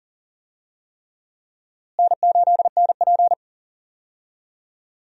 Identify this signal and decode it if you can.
Morse code, heard as a keyed tone, about 38 wpm, 700 Hz. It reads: N8NP
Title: USA Call Signs - 4 Characters - Repeated 3x - 40wpm